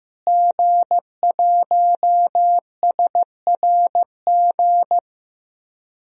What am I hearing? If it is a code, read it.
Morse code, heard as a keyed tone, about 15 wpm, 700 Hz. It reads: G1SRG
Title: DXCC-100 Call Signs - 5 Characters - Repeated 2x - 15wpm